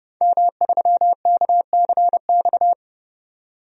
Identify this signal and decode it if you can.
Morse code, heard as a keyed tone, about 30 wpm, 700 Hz. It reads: M3KCX
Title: DXCC-100 Call Signs - 5 Characters - 30wpm